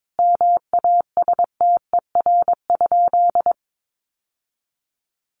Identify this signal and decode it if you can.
Morse code, heard as a keyed tone, about 22 wpm, 700 Hz. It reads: MASTER?
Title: Sentences from Top 1300 Words 22wpm